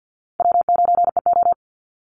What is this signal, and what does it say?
Morse code, heard as a keyed tone, about 50 wpm, 700 Hz. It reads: W9EO
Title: Call Signs Repeated 2x - 50wpm